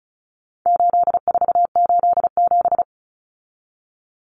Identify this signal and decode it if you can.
Morse code, heard as a keyed tone, about 34 wpm, 700 Hz. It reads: 8487
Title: Common Cut Numbers - 4-Digit 35wpm